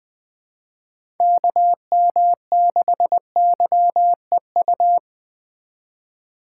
Morse code, heard as a keyed tone, about 24 wpm, 700 Hz. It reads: KM6YEU
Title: Speed Racing - USA Call Signs 20wpm